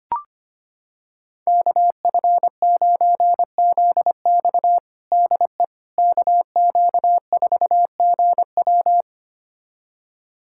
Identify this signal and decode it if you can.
Morse code, heard as a keyed tone, about 25 wpm, 700 Hz. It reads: KF9ZX DE KQ4GW
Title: USA Callsign DE Callsign - 5 Characters - 25wpm